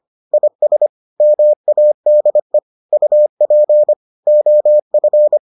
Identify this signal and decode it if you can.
Morse code, heard as a keyed tone, about 25 wpm, 600 Hz. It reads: IS MADE UP OF